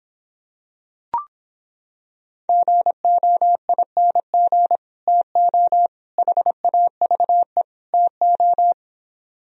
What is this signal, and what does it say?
Morse code, heard as a keyed tone, about 26 wpm, 700 Hz. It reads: GOING TO HAVE TO